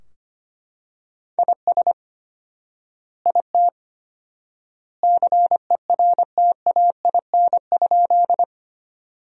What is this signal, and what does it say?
Morse code, heard as a keyed tone, about 25 wpm, 700 Hz. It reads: IS IT CERTAIN?
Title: 4x Word-Spacing - Sentences from Top 500 Words - 25wpm